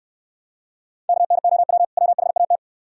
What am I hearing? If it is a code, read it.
Morse code, heard as a keyed tone, about 69 wpm, 700 Hz. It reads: KNOW WHAT